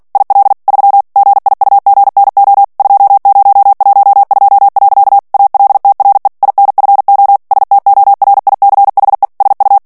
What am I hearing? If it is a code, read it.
Morse code, heard as a keyed tone, about 48 wpm, 800 Hz. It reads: IL 3 GIUGNO 2012. ALTRE INFO STORICHE SU
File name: morse mp3 / ITALIEN texte clair 1 vitesse 45 mots minute